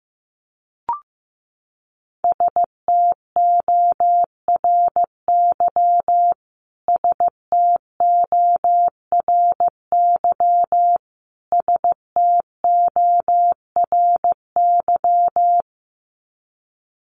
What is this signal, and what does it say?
Morse code, heard as a keyed tone, about 15 wpm, 700 Hz. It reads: STORY STORY STORY
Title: Top 500 Words Repeated 15wpm